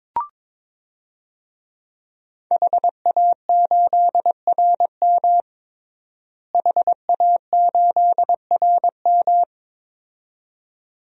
Morse code, heard as a keyed tone, about 22 wpm, 700 Hz. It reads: HA8RM HA8RM
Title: DXCC-100 Call Signs - 5 Characters - Repeated 2x - 22wpm